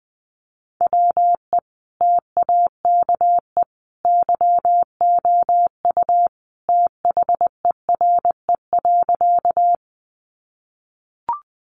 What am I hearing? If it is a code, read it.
Morse code, heard as a keyed tone, about 20 wpm, 700 Hz. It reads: WE TAKE YOU THERE.
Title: A Sound Of Thunder 20wpm